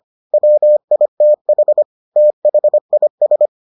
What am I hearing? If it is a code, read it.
Morse code, heard as a keyed tone, about 25 wpm, 600 Hz. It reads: WITH THIS